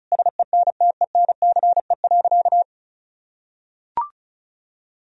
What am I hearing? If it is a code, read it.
Morse code, heard as a keyed tone, about 34 wpm, 700 Hz. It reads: SENTENCE.
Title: Sentences from Top 1000 Words - Review Part 1 - 35wpm